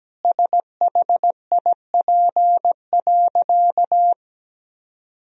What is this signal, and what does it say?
Morse code, heard as a keyed tone, about 17 wpm, 700 Hz. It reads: SHIP.
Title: Sentences from Top 1000 Words - Review Part 2 - 17wpm